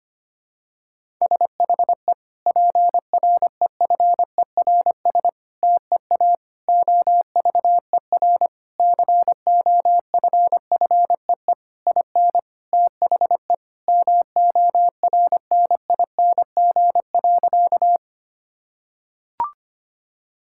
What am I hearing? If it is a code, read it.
Morse code, heard as a keyed tone, about 25 wpm, 700 Hz. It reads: SHE PREFERS TEA OVER COFFEE IN THE MORNING.
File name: practice / 25wpm / Difficult Sentences with Hints - 25wpm